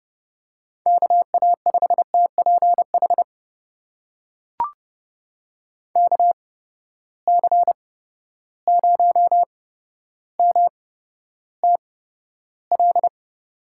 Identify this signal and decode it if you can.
Morse code, heard as a keyed tone, about 32 wpm, spaced as for 13 wpm, 700 Hz. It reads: KA5TPH KC0MTL
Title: USA Call Signs - 6 Characters - 8x Character Spacing - 30wpm